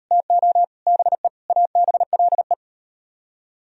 Morse code, heard as a keyed tone, about 36 wpm, 700 Hz. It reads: TO BE ABLE